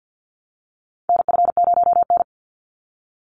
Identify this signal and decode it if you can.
Morse code, heard as a keyed tone, about 50 wpm, 700 Hz. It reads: NF0N
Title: Call Signs Repeated 3x - 50wpm